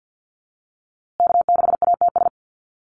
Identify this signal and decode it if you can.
Morse code, heard as a keyed tone, about 50 wpm, 700 Hz. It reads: K6ATS
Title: USA Call Signs Repeated 2x 50wpm